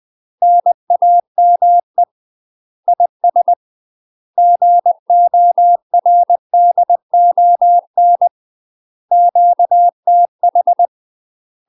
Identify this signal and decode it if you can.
Morse code, heard as a keyed tone, about 20 wpm, 700 Hz. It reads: NAME IS GORDON QTH
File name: UK MASTER QSOs TRAINING / 20WPM / UK MASTER QSOs 0029